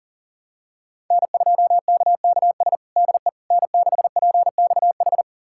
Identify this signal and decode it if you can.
Morse code, heard as a keyed tone, about 40 wpm, 700 Hz. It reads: N2KKS DE N6PXH
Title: USA Callsign DE Callsign - 5 Characters - 40wpm